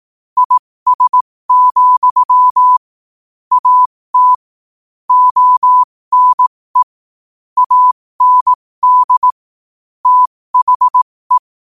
Morse code, heard as a keyed tone, about 18 wpm, spaced as for 15 wpm, 1000 Hz. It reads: IS, AT ONE AND THE
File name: qotdfiles / 15wpm